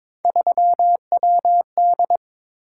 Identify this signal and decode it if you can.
Morse code, heard as a keyed tone, about 22 wpm, 700 Hz. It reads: 3WD
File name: practice / 22wpm / Three Letter-Number Combinations - 22wpm